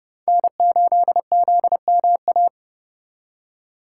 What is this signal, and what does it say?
Morse code, heard as a keyed tone, about 37 wpm, 700 Hz. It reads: N8ZMA
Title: Speed Racing - USA 1x3 Call Signs 30wpm